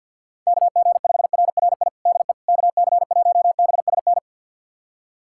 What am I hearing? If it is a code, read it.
Morse code, heard as a keyed tone, about 50 wpm, 700 Hz. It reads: KG5RRI DE KC1BSN
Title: USA Callsign DE Callsign 50wpm